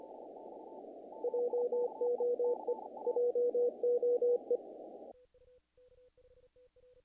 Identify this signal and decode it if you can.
Morse code, heard as a keyed tone, about 27 wpm, 490 Hz. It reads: JOE JOE
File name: contests / naqp-cw-k5tr-op-k5pi- -1800